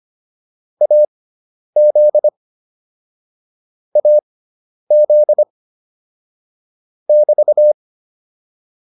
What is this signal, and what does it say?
Morse code, heard as a keyed tone, about 25 wpm, spaced as for 10 wpm, 600 Hz. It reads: AZ AZ <BT>